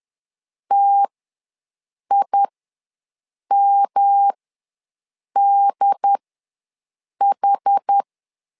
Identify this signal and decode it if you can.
Morse code, heard as a keyed tone, about 10 wpm, spaced as for 6 wpm, 800 Hz. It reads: TIMDH
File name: LEZIONI / 3 MWRO / 3 M ws 10wpm 2121